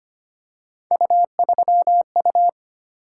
Morse code, heard as a keyed tone, about 25 wpm, 700 Hz. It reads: U3U
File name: Three Letter-Number Combinations - 25wpm